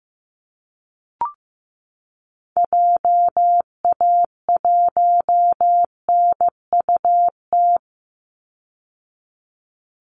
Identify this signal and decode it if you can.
Morse code, heard as a keyed tone, about 15 wpm, 700 Hz. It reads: JA1NUT
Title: Call Signs Repeated 2x - 15wpm